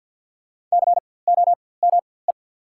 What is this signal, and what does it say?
Morse code, heard as a keyed tone, about 51 wpm, spaced as for 23 wpm, 690 Hz. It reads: COME
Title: 4x Character Spacing - Top 100 Words - 50wpm